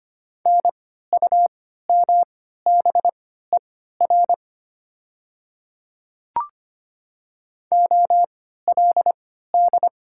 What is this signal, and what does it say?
Morse code, heard as a keyed tone, about 25 wpm, spaced as for 14 wpm, 700 Hz. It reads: NUMBER OLD
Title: 3x Character Spacing - Top 300 Words - 25wpm